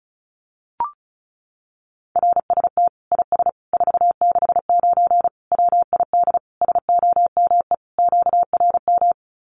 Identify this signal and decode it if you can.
Morse code, heard as a keyed tone, about 34 wpm, 700 Hz. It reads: RST IS 469 WID SOME QRM
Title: Common QSO Phrases 35wpm